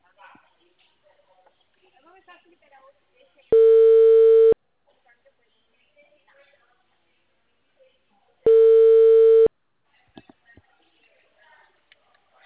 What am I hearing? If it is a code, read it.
Ringback tone (China, 450 Hz), with speech.